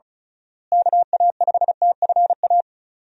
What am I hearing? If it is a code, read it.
Morse code, heard as a keyed tone, about 36 wpm, 700 Hz. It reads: KA5TFA
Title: USA Call Signs - 6 Characters - Repeated 2x - 35wpm